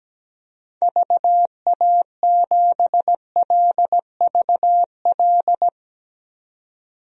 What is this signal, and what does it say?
Morse code, heard as a keyed tone, about 17 wpm, 700 Hz. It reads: VA7LVL
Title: DXCC-100 Call Signs - 6 Characters - 17wpm